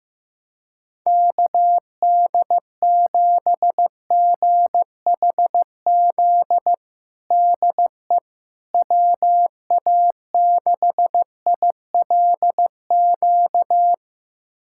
Morse code, heard as a keyed tone, about 15 wpm, 700 Hz. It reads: KD7GHZ DE WA6ILQ
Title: USA Callsign DE Callsign - 6 Characters - 15wpm